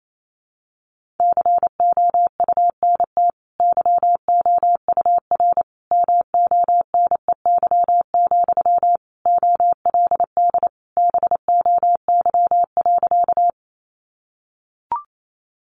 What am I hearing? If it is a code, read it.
Morse code, heard as a keyed tone, about 28 wpm, 700 Hz. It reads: COUNT YOUR MONEY, OLD BOY.
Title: Sentences from Top 1500 Words 28wpm